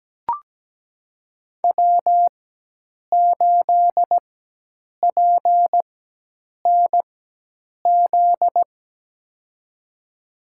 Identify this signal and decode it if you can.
Morse code, heard as a keyed tone, about 17 wpm, spaced as for 8 wpm, 700 Hz. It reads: W8PNZ
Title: USA Call Signs - 5 Characters - 4x Character Spacing - 17wpm